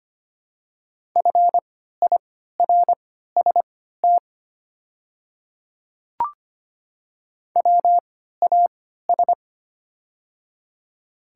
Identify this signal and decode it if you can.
Morse code, heard as a keyed tone, about 25 wpm, spaced as for 14 wpm, 700 Hz. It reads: FIRST WAS
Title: 3x Character Spacing - Top 100 Words - 25wpm